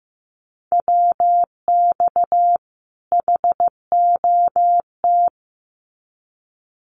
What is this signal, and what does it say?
Morse code, heard as a keyed tone, about 15 wpm, 700 Hz. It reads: WX HOT
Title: Common QSO Phrases 15wpm